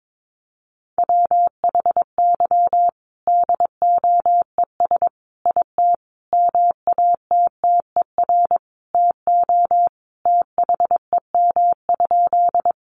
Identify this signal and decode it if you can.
Morse code, heard as a keyed tone, about 22 wpm, 700 Hz. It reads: WHY DOES IT MATTER TO THEM?
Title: Sentences from Top 700 Words 22wpm